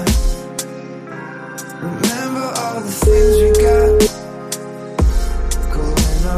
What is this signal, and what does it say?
Ringback tone (Europe, 425 Hz).